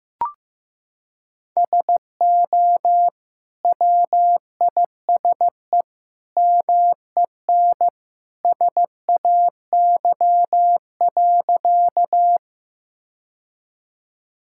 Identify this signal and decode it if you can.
Morse code, heard as a keyed tone, about 15 wpm, 700 Hz. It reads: SO WISE MEN SAY.
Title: Sentences from Top 1600 Words 15wpm